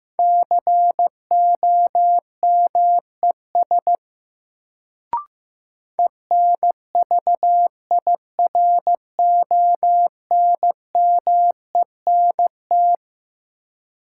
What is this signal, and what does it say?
Morse code, heard as a keyed tone, about 15 wpm, 700 Hz. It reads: COMES ENVIRONMENT
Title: Sentences from Top 1000 Words - Review Part 5 - New Word Drill - 15wpm